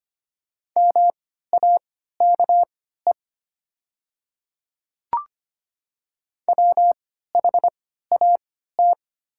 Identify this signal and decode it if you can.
Morse code, heard as a keyed tone, about 25 wpm, spaced as for 14 wpm, 700 Hz. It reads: MAKE WHAT
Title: 3x Character Spacing - Top 100 Words - 25wpm